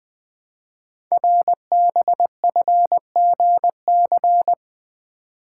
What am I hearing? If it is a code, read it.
Morse code, heard as a keyed tone, about 20 wpm, 700 Hz. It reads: RBFGC
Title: Code Groups - 5 Letters - 20wpm